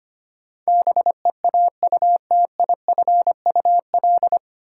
Morse code, heard as a keyed tone, about 25 wpm, 700 Hz. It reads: BEAUTIFUL
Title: Sentences from Top 900 Words - New Word Drill - 25wpm